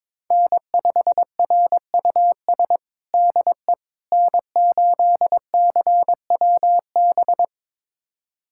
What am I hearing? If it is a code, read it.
Morse code, heard as a keyed tone, about 22 wpm, 700 Hz. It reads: N5RUS DE N8CWB
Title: USA Callsign DE Callsign - 5 Characters - 22wpm